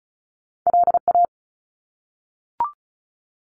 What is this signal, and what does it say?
Morse code, heard as a keyed tone, about 34 wpm, 700 Hz. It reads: LA
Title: ARRL Sections 35wpm